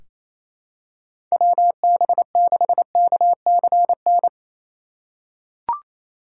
Morse code, heard as a keyed tone, about 28 wpm, 700 Hz. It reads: WB6KCN
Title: USA Call Signs - 6 Characters - 3x Character Spacing - 28wpm